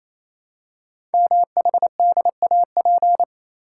Morse code, heard as a keyed tone, about 27 wpm, 700 Hz. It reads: MHDAP
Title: Code Groups - 5 Letters - 28wpm